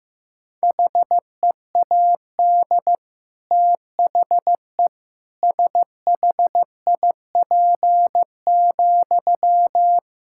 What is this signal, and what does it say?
Morse code, heard as a keyed tone, about 15 wpm, 700 Hz. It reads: HEAD THE SHIP,
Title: Sentences from Top 1000 Words - Review Part 5 - 15wpm